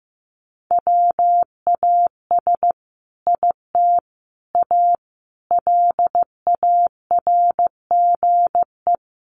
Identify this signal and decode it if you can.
Morse code, heard as a keyed tone, about 15 wpm, 700 Hz. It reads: WAS IT A LARGE
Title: Sentences from Top 1300 Words 15wpm